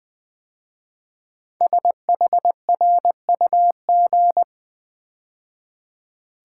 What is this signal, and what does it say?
Morse code, heard as a keyed tone, about 21 wpm, 700 Hz. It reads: SHRUG
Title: 5 Letter Words - First 3 Letters Given - 20wpm